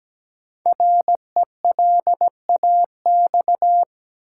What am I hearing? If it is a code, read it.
Morse code, heard as a keyed tone, about 17 wpm, 700 Hz. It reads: RELAX
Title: Sentences from Top 1400 Words - New Word Drill - 17wpm